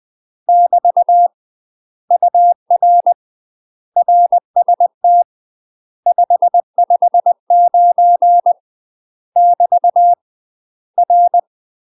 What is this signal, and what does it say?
Morse code, heard as a keyed tone, about 20 wpm, 700 Hz. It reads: <BT> UR RST 559 <BT> R
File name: UK MASTER QSOs TRAINING/20WPM